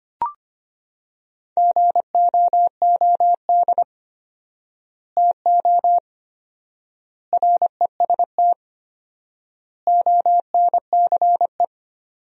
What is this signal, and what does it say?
Morse code, heard as a keyed tone, about 25 wpm, 700 Hz. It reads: GOOD TO REST ONCE
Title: 4x Word-Spacing - Sentences from Top 500 Words - 25wpm